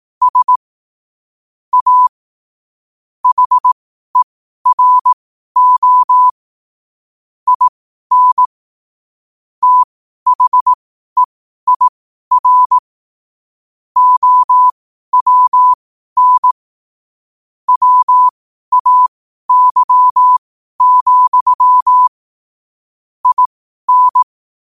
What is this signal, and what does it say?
Morse code, heard as a keyed tone, about 18 wpm, spaced as for 12 wpm, 1000 Hz. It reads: S A HERO IN THEIR OWN WAY, IN
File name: qotdfiles / 12wpm